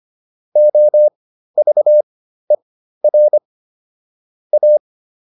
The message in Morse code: OVER A

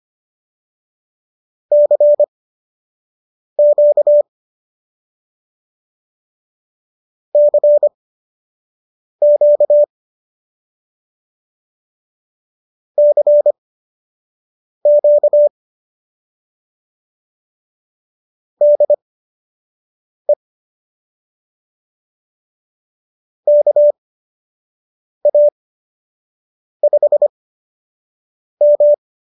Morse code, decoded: CQ CQ CQ DE KA5M

